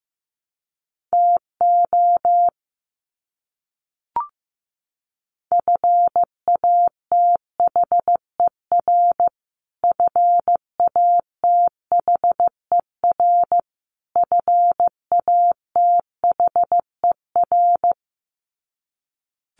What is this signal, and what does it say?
Morse code, heard as a keyed tone, about 15 wpm, 700 Hz. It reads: TO FATHER FATHER FATHER
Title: Top 300 Words Repeated 15wpm